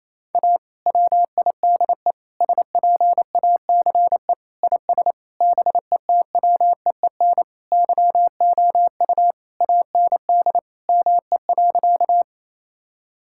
Morse code, heard as a keyed tone, about 28 wpm, 700 Hz. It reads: A WIDE SPACE IS BETWEEN YOU AND ME.
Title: Sentences from Top 1000 Words - Review Part 6 - 28wpm